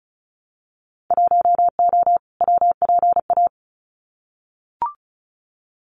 Morse code, heard as a keyed tone, about 34 wpm, 700 Hz. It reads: 1O WPA
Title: Winter Field Day Exchanges 35wpm